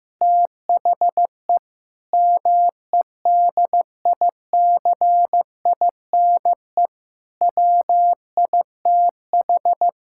Morse code, heard as a keyed tone, about 15 wpm, 700 Hz. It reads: THE MEDICINE WITH
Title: Sentences from Top 1000 Words - Review Part 6 - 15wpm